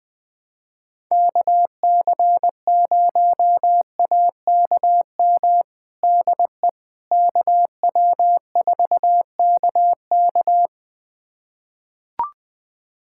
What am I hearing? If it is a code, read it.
Morse code, heard as a keyed tone, about 20 wpm, 700 Hz. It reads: KC0AKM DE KW4KK
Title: USA Callsign DE Callsign 20wpm